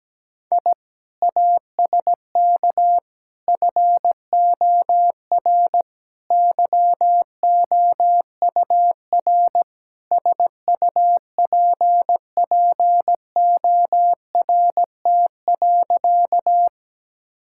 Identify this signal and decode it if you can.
Morse code, heard as a keyed tone, about 17 wpm, 700 Hz. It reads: I ASK FOR YOUR SUPPORT.
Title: Sentences from Top 1000 Words - Review Part 5 - 17wpm